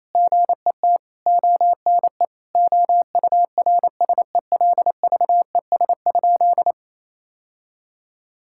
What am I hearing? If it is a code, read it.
Morse code, heard as a keyed tone, about 27 wpm, 700 Hz. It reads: GET ONE OURSELVES?
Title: Sentences from Top 1600 Words 28wpm